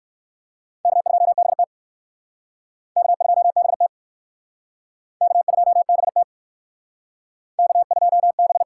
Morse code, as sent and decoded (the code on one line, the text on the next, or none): -.- ..--- -... - / -.- ..--- -... - / -.- ..--- -... - / -.- ..--- -...
K2BT K2BT K2BT K2B